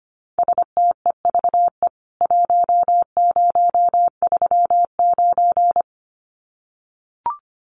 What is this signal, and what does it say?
Morse code, heard as a keyed tone, about 25 wpm, 700 Hz. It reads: STEVE 1039
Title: CWT Exchange 25wpm